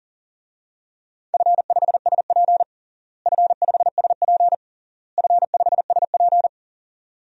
Morse code, heard as a keyed tone, about 40 wpm, 700 Hz. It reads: F5SP F5SP F5SP
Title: DXCC-100 Call Signs - 4 Characters - Repeated 3x - 40wpm